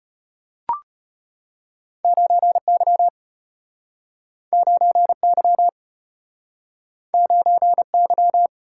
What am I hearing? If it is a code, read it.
Morse code, heard as a keyed tone, about 33 wpm, 700 Hz. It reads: 9Y 9Y 9Y